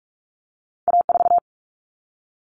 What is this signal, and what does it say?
Morse code, heard as a keyed tone, about 46 wpm, 700 Hz. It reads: A4
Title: Common Cut Numbers - 2-Digit 45wpm